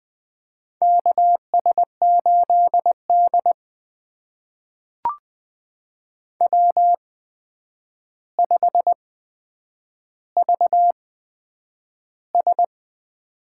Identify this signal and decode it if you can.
Morse code, heard as a keyed tone, about 20 wpm, spaced as for 9 wpm, 700 Hz. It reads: KS8D W5VS
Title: USA Call Signs - 4 Characters - 8x Character Spacing - 20wpm